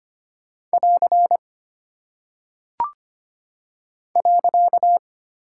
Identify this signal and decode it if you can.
Morse code, heard as a keyed tone, about 25 wpm, 700 Hz. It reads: <AR> .
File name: Single Punctuation-Prosign - 25wpm